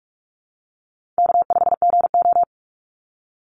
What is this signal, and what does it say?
Morse code, heard as a keyed tone, about 44 wpm, 700 Hz. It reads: K5GO
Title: Call Signs Repeated 2x - 45wpm